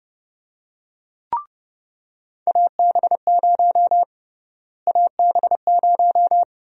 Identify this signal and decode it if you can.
Morse code, heard as a keyed tone, about 31 wpm, 700 Hz. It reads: AB0 AB0
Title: USA Call Sign Prefixes - Repeated 2x - 30wpm